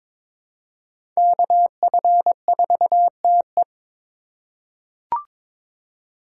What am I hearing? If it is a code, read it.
Morse code, heard as a keyed tone, about 22 wpm, 700 Hz. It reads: KF4TE
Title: USA Call Signs - 3x Character Spacing - 22wpm